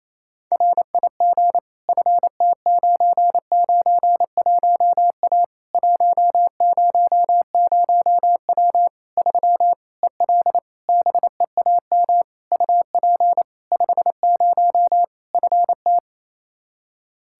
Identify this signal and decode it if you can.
Morse code, heard as a keyed tone, about 28 wpm, 700 Hz. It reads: RIG FT991A 100W 3 EL BEAM UP 50 FT
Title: K1USN SST QSO Practice 28wpm